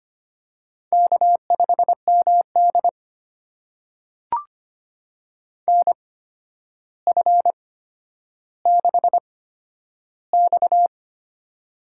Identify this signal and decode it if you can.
Morse code, heard as a keyed tone, about 25 wpm, spaced as for 11 wpm, 700 Hz. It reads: K5MD NF6X
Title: USA Call Signs - 4 Characters - 8x Character Spacing - 25wpm